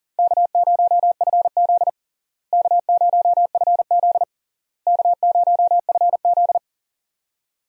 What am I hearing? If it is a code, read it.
Morse code, heard as a keyed tone, about 37 wpm, 700 Hz. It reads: K0FZ K0FZ K0FZ
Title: USA Call Signs - 4 Characters - Repeated 3x - 40wpm